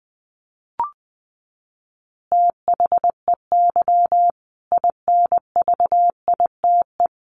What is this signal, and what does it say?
Morse code, heard as a keyed tone, about 20 wpm, 700 Hz. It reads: THEY INVITE
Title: Easy Phrases 20wpm